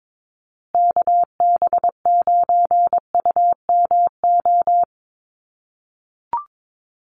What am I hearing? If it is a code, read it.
Morse code, heard as a keyed tone, about 22 wpm, 700 Hz. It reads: KB9UMO
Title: USA Call Signs Repeated 2x 22wpm